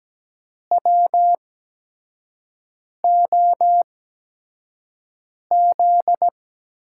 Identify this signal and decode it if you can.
Morse code, heard as a keyed tone, about 17 wpm, 700 Hz. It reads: W O Z
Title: USA Call Signs - 6 Characters - 8x Character Spacing - 17wpm